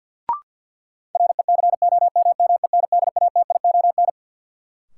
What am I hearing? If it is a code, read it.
Morse code, heard as a keyed tone, about 51 wpm, 700 Hz. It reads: RECOMMENDATION
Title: Sentences from Top 1400 Words - New Word Drill - 50wpm